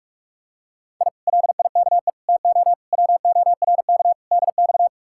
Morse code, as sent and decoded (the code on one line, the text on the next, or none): .. / .-.. .. -.- . / - --- / .-- --- .-. -.- / -.. -..-
I LIKE TO WORK DX